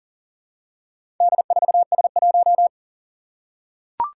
Morse code, heard as a keyed tone, about 40 wpm, 700 Hz. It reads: D4S1
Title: Four Challenging Letter-Number Combinations 40wpm